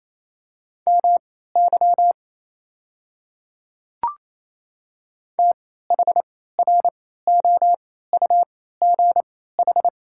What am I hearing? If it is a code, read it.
Morse code, heard as a keyed tone, about 27 wpm, spaced as for 16 wpm, 700 Hz. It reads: MY THROUGH
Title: 3x Character Spacing - Top 200 Words - 28wpm